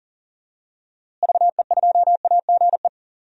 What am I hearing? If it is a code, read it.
Morse code, heard as a keyed tone, about 39 wpm, 700 Hz. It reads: VE2AGE
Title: DXCC-100 Call Signs - 6 Characters - Repeated 3x - 40wpm